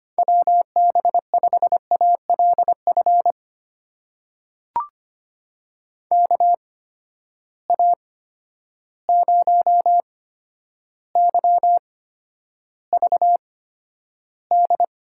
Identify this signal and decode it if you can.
Morse code, heard as a keyed tone, about 25 wpm, spaced as for 11 wpm, 700 Hz. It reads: WB5ALF KA0YVD
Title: USA Call Signs - 6 Characters - 8x Character Spacing - 25wpm